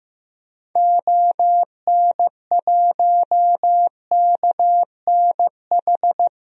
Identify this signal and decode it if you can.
Morse code, heard as a keyed tone, about 15 wpm, 700 Hz. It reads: ON1KNH
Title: DXCC-100 Call Signs - 6 Characters - 15wpm